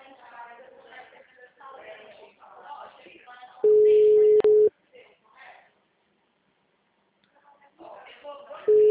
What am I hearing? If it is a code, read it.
Ringback tone (Europe, 425 Hz), with speech.